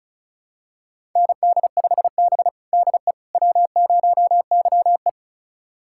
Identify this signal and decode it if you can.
Morse code, heard as a keyed tone, about 36 wpm, 700 Hz. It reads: ND5B DE W0YE
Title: USA Callsign DE Callsign - 4 Characters - 35wpm